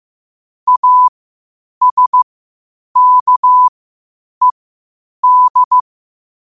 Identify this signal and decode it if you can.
Morse code, heard as a keyed tone, about 15 wpm, spaced as for 8 wpm, 1000 Hz. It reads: ASKED
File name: qotdfiles / 8wpm